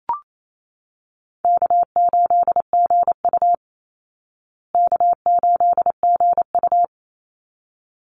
Morse code, heard as a keyed tone, about 28 wpm, 700 Hz. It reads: K8GU K8GU
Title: Easy Call Signs Repeated 2x - 28wpm